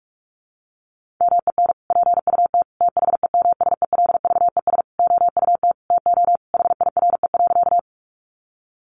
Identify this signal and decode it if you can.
Morse code, heard as a keyed tone, about 44 wpm, 700 Hz. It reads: MEN PUT THEMSELVES OUT TO HIRE.
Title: Sentences from Top 1400 Words 45wpm